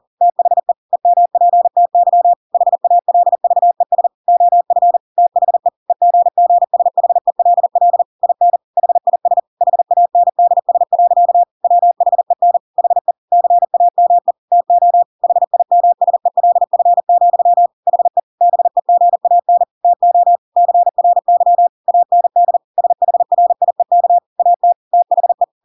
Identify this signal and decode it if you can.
Morse code, heard as a keyed tone, about 40 wpm, 700 Hz. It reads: THE EMPTY HALVES OF THE EGGSHELL IN HIS HANDS. WHEN HE CAME TO HIMSELF, HE BEGAN TO CRY AND SHRIEK AT THE
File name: Pinocchio / Pinocchio ch5 40 40 ik1hge